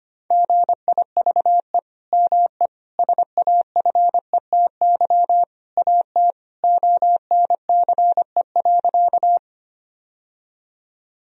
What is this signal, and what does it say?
Morse code, heard as a keyed tone, about 25 wpm, 700 Hz. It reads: GIVE ME SAFETY AT ONCE.
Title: Sentences from Top 1000 Words - Review Part 4 - 25wpm